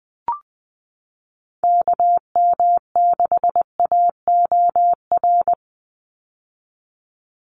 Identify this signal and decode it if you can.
Morse code, heard as a keyed tone, about 20 wpm, 700 Hz. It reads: KM6AOR
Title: USA Call Signs 20wpm